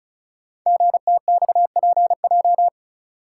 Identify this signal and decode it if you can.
Morse code, heard as a keyed tone, about 36 wpm, 700 Hz. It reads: GTXPJ
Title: Code Groups - 5 Letters - 35wpm